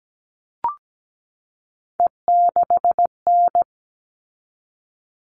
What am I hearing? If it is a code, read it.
Morse code, heard as a keyed tone, about 17 wpm, 700 Hz. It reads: E6N
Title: All Cut Numbers - 3-Digit 17wpm